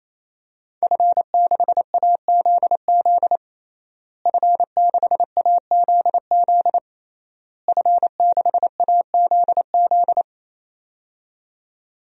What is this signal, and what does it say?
Morse code, heard as a keyed tone, about 28 wpm, 700 Hz. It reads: F6AZZ F6AZZ F6AZZ
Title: DXCC-100 Call Signs - 5 Characters - Repeated 3x - 28wpm